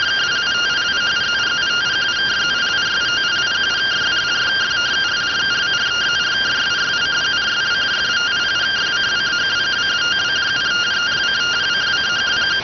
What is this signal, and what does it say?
Two-tone teleprinter signal, about 170 Hz shift.